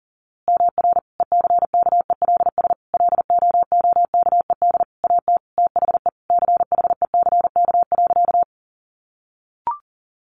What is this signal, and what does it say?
Morse code, heard as a keyed tone, about 40 wpm, 700 Hz. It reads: MR ECKELS LOOKED AT THE CHECK.
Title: A Sound Of Thunder 40wpm